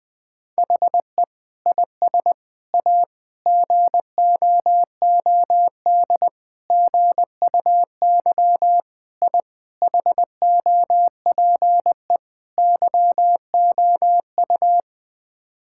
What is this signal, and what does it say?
Morse code, heard as a keyed tone, about 21 wpm, 700 Hz. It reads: HE IS A GOOD GUY I HOPE YOU
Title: Sets of 8 Words - 20wpm